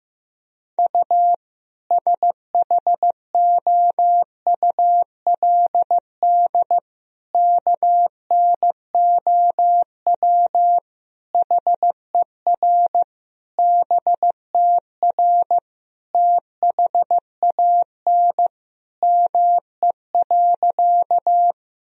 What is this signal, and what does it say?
Morse code, heard as a keyed tone, about 15 wpm, 700 Hz. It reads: U SHOULD KNOW HER BTR THAN ME.
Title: Sentences from Top 400 Words and Abbreviations 15wpm